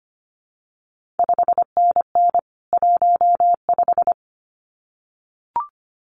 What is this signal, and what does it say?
Morse code, heard as a keyed tone, about 25 wpm, 700 Hz. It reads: 5NN 15
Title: CQ WW DX Contest Exchange - 25wpm